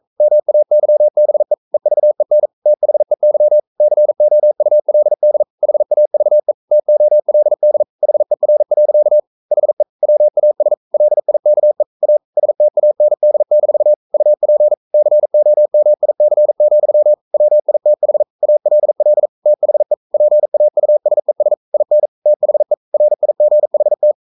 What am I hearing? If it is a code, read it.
Morse code, heard as a keyed tone, about 42 wpm, 600 Hz. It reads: MAYBE EVEN THEY COULD HAVE TOLD HER. HE WAS LIKE A STAND-UP COMIC, WITH ALL THE PAUSES IN THE RIGHT
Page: Romanen Never af Ken Follet på engelsk.